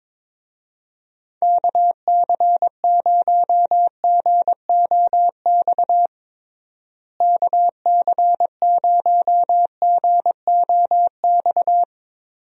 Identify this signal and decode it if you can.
Morse code, heard as a keyed tone, about 22 wpm, 700 Hz. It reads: KC0GOX KC0GOX
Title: USA Call Signs - 6 Characters - Repeated 3x - 22wpm